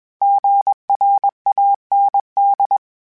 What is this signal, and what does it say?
Morse code, heard as a keyed tone, about 21 wpm, 800 Hz. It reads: GRAND
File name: fromSMs / FromSMs mp3s / Cities@20